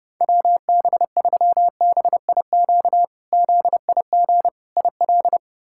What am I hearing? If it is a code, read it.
Morse code, heard as a keyed tone, about 32 wpm, 700 Hz. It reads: WB3BIQ ZIG IL
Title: K1USN SST QSO Practice - 30wpm